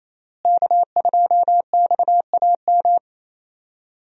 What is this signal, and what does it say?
Morse code, heard as a keyed tone, about 28 wpm, 700 Hz. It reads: K2XAM
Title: USA Call Signs - 5 Characters - Repeated 3x - 28wpm